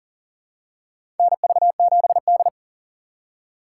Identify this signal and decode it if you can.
Morse code, heard as a keyed tone, about 40 wpm, 700 Hz. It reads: NV7D
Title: USA Call Signs - 4 Characters - 2x Character Spacing - 40wpm